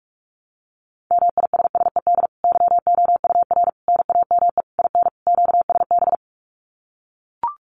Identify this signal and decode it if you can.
Morse code, heard as a keyed tone, about 44 wpm, 700 Hz. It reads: MISSED YOUR NAME IN QSB
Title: Common QSO Phrases 45wpm